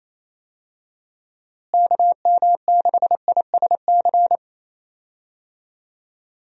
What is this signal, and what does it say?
Morse code, heard as a keyed tone, about 27 wpm, 700 Hz. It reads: KM6ISC
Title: USA Call Signs - 6 Characters - Repeated 3x - 28wpm